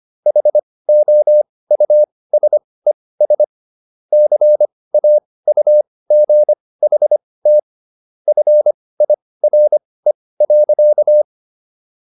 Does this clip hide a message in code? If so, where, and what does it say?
Yes: Morse code from 0.3 to 11.2 s, heard as a keyed tone, about 25 wpm, spaced as for 18 wpm, 600 Hz. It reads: HOUSES CAUGHT FIRE.